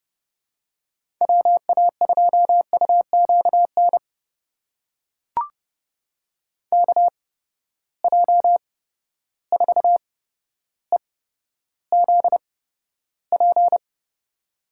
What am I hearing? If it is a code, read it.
Morse code, heard as a keyed tone, about 30 wpm, spaced as for 13 wpm, 700 Hz. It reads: WA2UQN KJ4EZP
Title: USA Call Signs - 8x Character Spacing - 30wpm